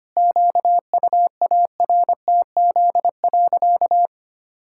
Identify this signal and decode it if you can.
Morse code, heard as a keyed tone, about 25 wpm, 700 Hz. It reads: QUARTZ.
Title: Pangram Challenge v2 - 25wpm